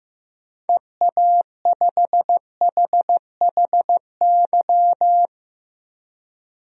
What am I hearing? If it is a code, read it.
Morse code, heard as a keyed tone, about 15 wpm, 700 Hz. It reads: EA5HHY
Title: DXCC-100 Call Signs - 6 Characters - 15wpm